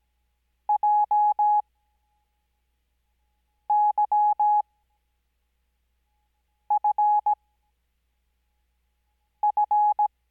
Morse code, heard as a keyed tone, about 17 wpm, spaced as for 4 wpm, 840 Hz. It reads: JYFF
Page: Farnsworth Spacing: 15WPM Character speed Spaced at 5WPM.